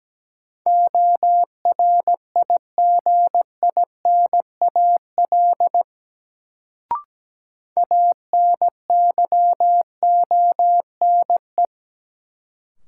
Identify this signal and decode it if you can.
Morse code, heard as a keyed tone, about 17 wpm, 700 Hz. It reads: ORIGINAL ANYONE
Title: Sentences from Top 900 Words - New Word Drill - 17wpm